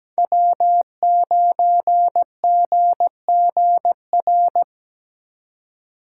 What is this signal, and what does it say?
Morse code, heard as a keyed tone, about 17 wpm, 700 Hz. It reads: W9GGR
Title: USA Call Signs - 5 Characters - 4x Character Spacing - 17wpm